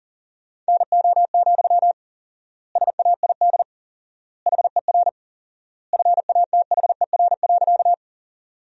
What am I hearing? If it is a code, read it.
Morse code, heard as a keyed tone, about 40 wpm, 700 Hz. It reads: NO, SAID HER FATHER.
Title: 4x Word-Spacing - Sentences from Top 300 Words - 40wpm